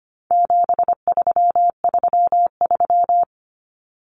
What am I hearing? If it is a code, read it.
Morse code, heard as a keyed tone, about 25 wpm, 700 Hz. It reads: 7333
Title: Numbers - 4-Digit 25wpm